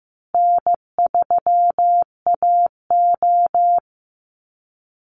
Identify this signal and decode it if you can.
Morse code, heard as a keyed tone, about 15 wpm, 700 Hz. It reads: N3AO
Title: Easy Call Signs Repeated 2x - 15wpm